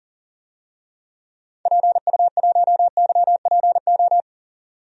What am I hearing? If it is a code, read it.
Morse code, heard as a keyed tone, about 40 wpm, 700 Hz. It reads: PU1YPO
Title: DXCC-100 Call Signs - 6 Characters - 40wpm